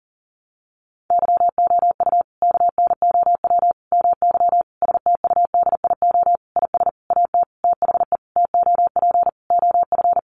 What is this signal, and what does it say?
Morse code, heard as a keyed tone, about 40 wpm, 700 Hz. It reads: YOU KNOW MY STUDIO IS AT THE TOP OF
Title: Sentences from Top 1200 Words 40wpm